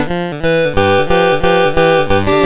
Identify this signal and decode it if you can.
There is no code.